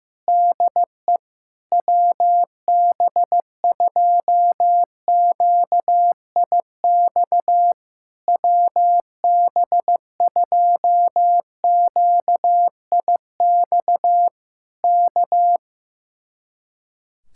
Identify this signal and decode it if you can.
Morse code, heard as a keyed tone, about 15 wpm, 700 Hz. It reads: DE WB2QIX WB2QIX K
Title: Speed-Racing Common QSO Phrases Encore - 15wpm